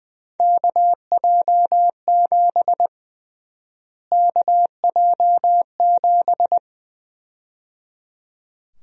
Morse code, heard as a keyed tone, about 20 wpm, 700 Hz. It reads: KJ7 KJ7
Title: USA Call Sign Prefixes - Repeated 2x - 20wpm